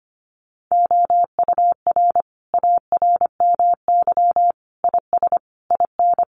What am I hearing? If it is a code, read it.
Morse code, heard as a keyed tone, about 25 wpm, 700 Hz. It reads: OUR ARMY IS IN
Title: Sentences from Top 1500 Words 25wpm